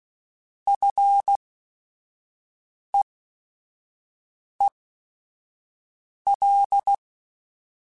Morse code, heard as a keyed tone, about 16 wpm, spaced as for 5 wpm, 800 Hz. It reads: FEEL
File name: Short Runs / Short Runs-mp3s / SinglePrefixes @5